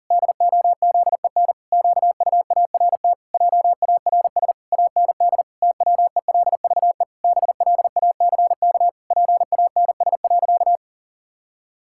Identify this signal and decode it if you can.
Morse code, heard as a keyed tone, about 40 wpm, 700 Hz. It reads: DOZEN QUART JARS AND TWELVE BLACK PANS.
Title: Pangram Challenge v2 - 40wpm